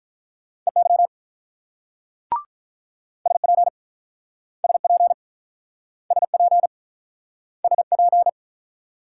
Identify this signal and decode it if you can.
Morse code, heard as a keyed tone, about 47 wpm, 700 Hz. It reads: EY SP SP SP SP